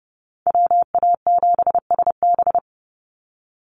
Morse code, heard as a keyed tone, about 30 wpm, 700 Hz. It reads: WA7SB
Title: NA SOTA Chaser Callsigns 30wpm